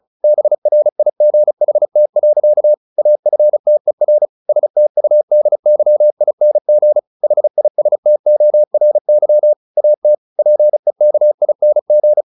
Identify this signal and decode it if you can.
Morse code, heard as a keyed tone, about 36 wpm, 600 Hz. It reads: BRIGHT. AFTER STUDYING HISTORY AT PEKING